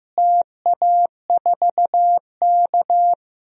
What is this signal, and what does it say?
Morse code, heard as a keyed tone, about 15 wpm, 700 Hz. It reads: TA4K
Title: DXCC-100 Call Signs - 4 Characters - Repeated 2x - 15wpm